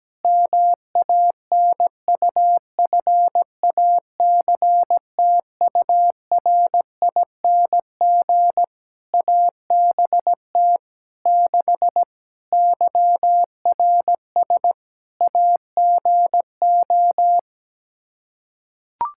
Morse code, heard as a keyed tone, about 17 wpm, 700 Hz. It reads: MANUFACTURING ABT 6 YRS AGO
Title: Practice QSO - Exchange 3 - More Info - 17wpm